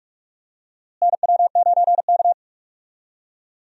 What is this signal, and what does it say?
Morse code, heard as a keyed tone, about 47 wpm, 700 Hz. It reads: NW9K
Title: USA Call Signs - 4 Characters - Repeated 2x - 45wpm